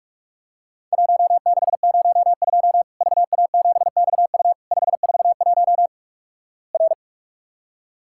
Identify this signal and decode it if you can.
Morse code, heard as a keyed tone, about 46 wpm, 700 Hz. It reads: 1602 VA7XU 541 R
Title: Worked All Europe DX Contest - QTC Exchange - 45wpm